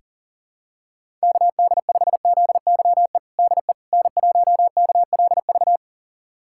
Morse code, heard as a keyed tone, about 40 wpm, 700 Hz. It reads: KD5ZYE DE N1KLV
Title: USA Callsign DE Callsign 40wpm